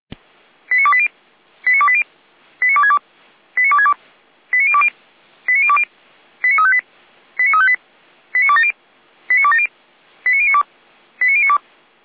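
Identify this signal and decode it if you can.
Five-tone selective call (ZVEI), heard as quick runs of five tones, tones of about 70 ms. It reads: 89280 89280 89262 89262 89020 89020 89358 89358 89270 89270 89092 89092
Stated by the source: Five-tone ZVEI selective call received on an SDR.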